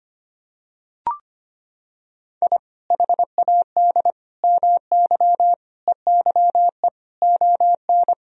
Morse code, heard as a keyed tone, about 25 wpm, 700 Hz. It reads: I HAD MY EYE ON
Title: Sentences from Top 1000 Words - Review Part 1 - 25wpm